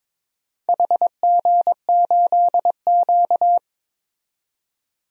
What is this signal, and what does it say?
Morse code, heard as a keyed tone, about 22 wpm, 700 Hz. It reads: HG8Q
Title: DXCC-100 Call Signs - 4 Characters - 22wpm